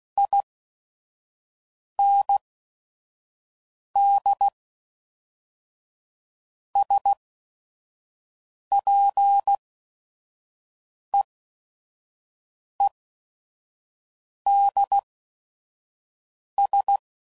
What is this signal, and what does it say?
Morse code, heard as a keyed tone, about 16 wpm, spaced as for 5 wpm, 800 Hz. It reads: IND SPEEDS
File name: CW Code Groups @5WPM / Weather Sets / Weather 4 14